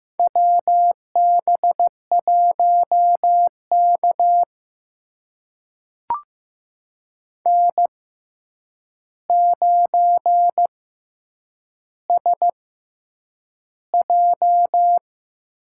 Morse code, heard as a keyed tone, about 15 wpm, spaced as for 8 wpm, 700 Hz. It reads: WB1K N9SJ
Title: USA Call Signs - 4 Characters - 6x Character Spacing - 15wpm